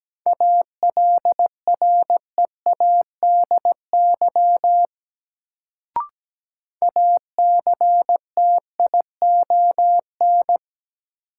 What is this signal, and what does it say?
Morse code, heard as a keyed tone, about 17 wpm, 700 Hz. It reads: ALREADY ACTION
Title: Sentences from Top 700 Words - New Word Drill - 17wpm